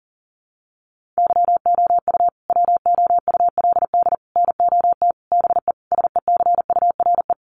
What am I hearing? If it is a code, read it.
Morse code, heard as a keyed tone, about 40 wpm, 700 Hz. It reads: YOU WOULD NOT BE SECURE
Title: Sentences from Top 1200 Words 40wpm